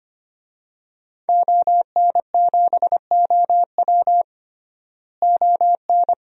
Morse code, heard as a keyed tone, about 25 wpm, 700 Hz. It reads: ON7OW ON
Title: DXCC-100 Call Signs - 5 Characters - Repeated 3x - 25wpm